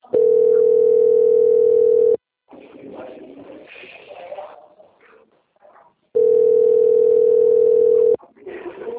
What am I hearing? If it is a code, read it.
Ringback tone (North America), 440 + 480 Hz.
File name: out- - - - - .9765